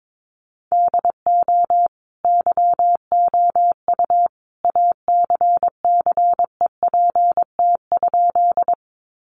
Morse code, heard as a keyed tone, about 22 wpm, 700 Hz. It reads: DO YOU ACCEPT?
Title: Sentences from Top 1300 Words 22wpm